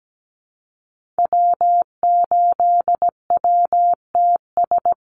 Morse code, heard as a keyed tone, about 17 wpm, 700 Hz. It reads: W8WTS
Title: Call Signs Repeated 2x - 17wpm